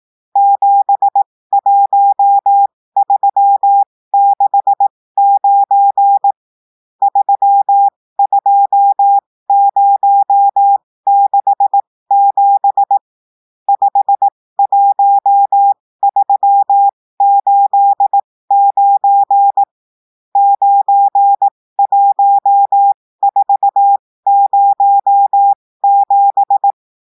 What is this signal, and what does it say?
Morse code, heard as a keyed tone, about 18 wpm, spaced as for 15 wpm, 800 Hz. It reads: 71369 32067 51389 91407
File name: cours-du-lundi files / cours-1580-texte-chiffre-15-mpm